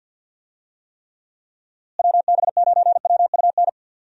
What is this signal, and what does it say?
Morse code, heard as a keyed tone, about 51 wpm, 700 Hz. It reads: WB9WUN
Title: USA Call Signs - 6 Characters - Repeated 2x - 50wpm